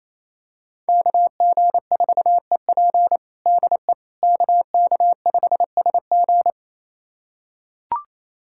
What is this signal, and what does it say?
Morse code, heard as a keyed tone, about 28 wpm, 700 Hz. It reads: KG4EP DE KK5SG
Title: USA Callsign DE Callsign - 5 Characters - 28wpm